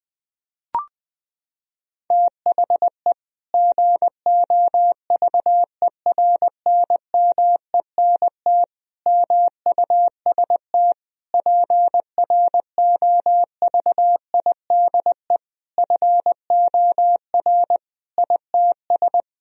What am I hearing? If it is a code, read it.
Morse code, heard as a keyed tone, about 20 wpm, 700 Hz. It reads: THE GOVERNMENT MUST PROVIDE FOR ITS
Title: Sentences from Top 1000 Words - Review Part 3 - 20wpm